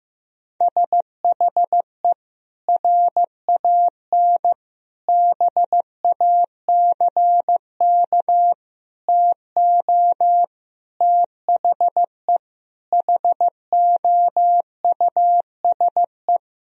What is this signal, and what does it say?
Morse code, heard as a keyed tone, about 15 wpm, 700 Hz. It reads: SHE RAN BACK TO THE HOUSE
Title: Sentences from Top 1000 Words - Review Part 3 - 15wpm